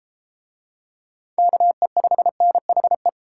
Morse code, heard as a keyed tone, about 30 wpm, 700 Hz. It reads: KE5NHE